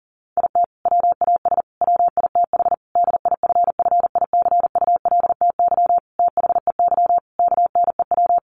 Morse code, heard as a keyed tone, about 37 wpm, 700 Hz. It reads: IT WAS WITH DIFFICULTY THEY KNEW
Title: Sentences from Top 1500 Words 40wpm